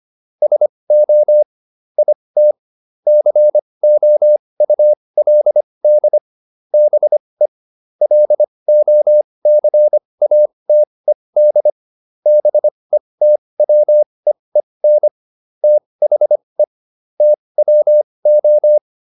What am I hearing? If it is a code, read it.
Morse code, heard as a keyed tone, about 25 wpm, spaced as for 20 wpm, 600 Hz. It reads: SO IT COULD BE LOCATED BETWEEN THE TWO